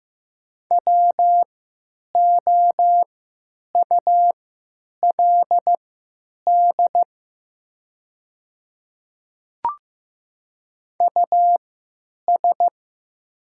Morse code, heard as a keyed tone, about 15 wpm, spaced as for 9 wpm, 690 Hz. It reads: WOULD US
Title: 3x Character Spacing - Top 300 Words - 15wpm